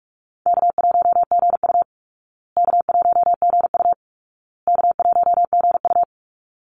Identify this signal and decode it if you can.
Morse code, heard as a keyed tone, about 44 wpm, 700 Hz. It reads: K1GU K1GU K1GU
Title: Call Signs Repeated 2x - 45wpm